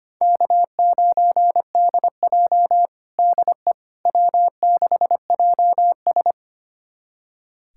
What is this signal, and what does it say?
Morse code, heard as a keyed tone, about 25 wpm, 700 Hz. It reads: K9DJ DE W6JS
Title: USA Callsign DE Callsign - 4 Characters - 25wpm